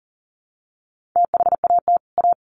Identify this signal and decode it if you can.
Morse code, heard as a keyed tone, about 40 wpm, 700 Hz. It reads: THAT A
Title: Sets of 2 Words 40wpm